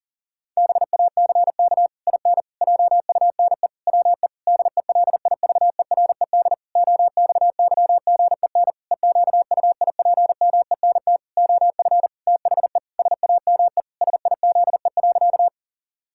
Morse code, heard as a keyed tone, about 40 wpm, 700 Hz. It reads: BACK IN JUNE WE DELIVERED OXYGEN EQUIPMENT OF THE SAME SIZE.
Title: Pangram Challenge v2 - 40wpm